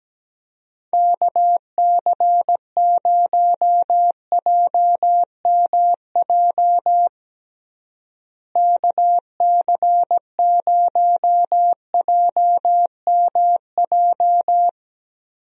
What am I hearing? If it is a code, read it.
Morse code, heard as a keyed tone, about 17 wpm, 700 Hz. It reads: KC0JMJ KC0JMJ
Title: USA Call Signs - 6 Characters - Repeated 3x - 17wpm